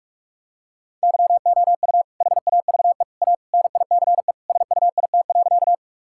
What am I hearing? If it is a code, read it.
Morse code, heard as a keyed tone, about 46 wpm, 700 Hz. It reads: YOU HAVE A NICE SUIT.
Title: Sentences from Top 1600 Words 45wpm